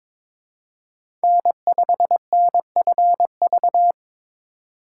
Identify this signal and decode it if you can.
Morse code, heard as a keyed tone, about 22 wpm, 700 Hz. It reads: N5NFV
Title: USA Call Signs - 5 Characters - 3x Character Spacing - 22wpm